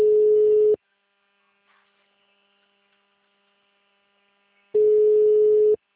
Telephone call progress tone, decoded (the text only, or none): ringback tone